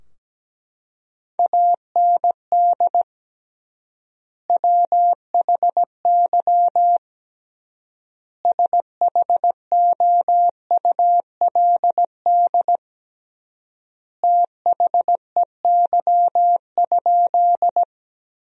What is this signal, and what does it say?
Morse code, heard as a keyed tone, about 17 wpm, 700 Hz. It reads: AND WHY SHOULD THEY?
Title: 3x Word-Spacing - Sentences from Top 300 Words - 17wpm